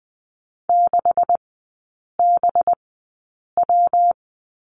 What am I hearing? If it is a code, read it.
Morse code, heard as a keyed tone, about 20 wpm, spaced as for 8 wpm, 700 Hz. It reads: 6BW
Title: 184 Lesson 30e - Letter b - Three Characters 20wpm